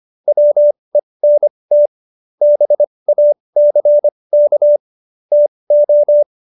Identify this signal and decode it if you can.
Morse code, heard as a keyed tone, about 25 wpm, spaced as for 20 wpm, 590 Hz. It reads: WENT BACK TO